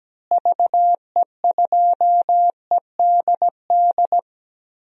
Morse code, heard as a keyed tone, about 17 wpm, 700 Hz. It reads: VE2EDD
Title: DXCC-100 Call Signs - 6 Characters - Repeated 2x - 17wpm